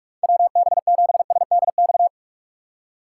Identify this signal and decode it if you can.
Morse code, heard as a keyed tone, about 56 wpm, 700 Hz. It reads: WB7SDX
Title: Speed Racing - USA 2x3 Call Signs 45wpm